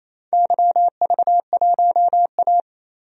Morse code, heard as a keyed tone, about 28 wpm, 700 Hz. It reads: YV1A
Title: DXCC-100 Call Signs - 4 Characters - 28wpm